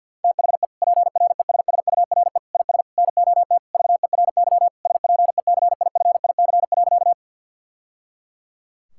Morse code, heard as a keyed tone, about 50 wpm, 700 Hz. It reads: THE PRESSURE IS NOT VERY SPECIFIC.
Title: Sentences from Top 1000 Words - Review Part 6 - 50wpm